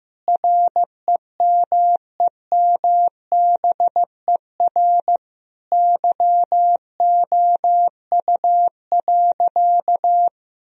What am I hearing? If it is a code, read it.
Morse code, heard as a keyed tone, about 15 wpm, 700 Hz. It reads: REMEMBER YOU.
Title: Sentences from Top 1000 Words - Review Part 2 - 15wpm